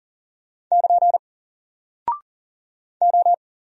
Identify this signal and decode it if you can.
Morse code, heard as a keyed tone, about 41 wpm, 700 Hz. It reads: <KN> O
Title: Single Character (Letter:Number:Punctuation:Prosign) 40wpm